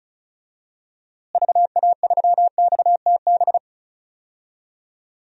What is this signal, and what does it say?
Morse code, heard as a keyed tone, about 36 wpm, 700 Hz. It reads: VA3XTB
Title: DXCC-100 Call Signs - 6 Characters - Repeated 3x - 35wpm